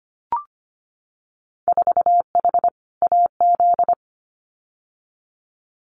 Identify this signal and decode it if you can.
Morse code, heard as a keyed tone, about 25 wpm, 700 Hz. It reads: 4H AZ
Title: Winter Field Day Exchanges Encore 25wpm